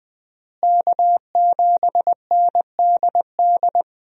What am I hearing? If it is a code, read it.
Morse code, heard as a keyed tone, about 21 wpm, 700 Hz. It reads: K7NDD
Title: USA Call Signs - 5 Characters - Repeated 3x - 20wpm